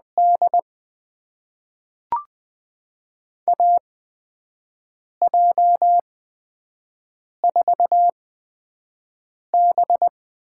Morse code, heard as a keyed tone, about 20 wpm, spaced as for 5 wpm, 700 Hz. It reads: D AJ4B
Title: USA Call Signs - 8x Character Spacing - 20wpm